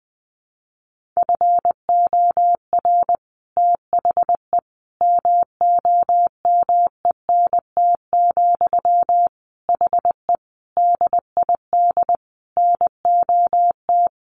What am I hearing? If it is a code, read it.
Morse code, heard as a keyed tone, about 20 wpm, 700 Hz. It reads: FOR THE MOMENT, HE DID NOT
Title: Sentences from Top 1400 Words 20wpm